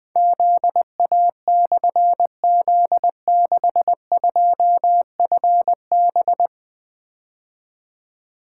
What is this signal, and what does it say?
Morse code, heard as a keyed tone, about 20 wpm, 700 Hz. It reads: ZA/Z62FB
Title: Difficult Call Signs 20wpm